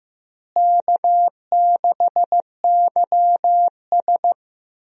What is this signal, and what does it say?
Morse code, heard as a keyed tone, about 15 wpm, 700 Hz. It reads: K6YS
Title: USA Call Signs - 4 Characters - Repeated 2x - 15wpm